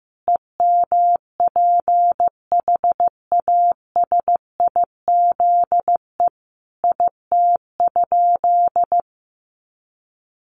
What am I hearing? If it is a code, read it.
Morse code, heard as a keyed tone, about 15 wpm, 700 Hz. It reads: EMPHASIZE IT?
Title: Sentences from Top 1500 Words 15wpm